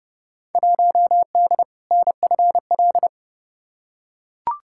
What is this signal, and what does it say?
Morse code, heard as a keyed tone, about 30 wpm, 700 Hz. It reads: 1D NFL
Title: Field Day Exchanges - 30wpm